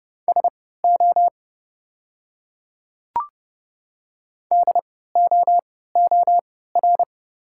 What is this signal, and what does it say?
Morse code, heard as a keyed tone, about 31 wpm, spaced as for 17 wpm, 690 Hz. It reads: SO DOOR
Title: 3x Character Spacing - Top 300 Words - 30wpm